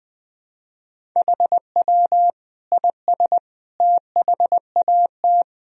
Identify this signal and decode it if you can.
Morse code, heard as a keyed tone, about 21 wpm, 700 Hz. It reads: HW IS THAT
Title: Sentences from Top 500 Words and Abbreviations 20wpm